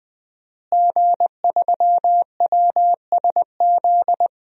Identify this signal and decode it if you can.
Morse code, heard as a keyed tone, about 20 wpm, 700 Hz. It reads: G3WSZ
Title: DXCC-100 Call Signs - 5 Characters - Repeated 3x - 20wpm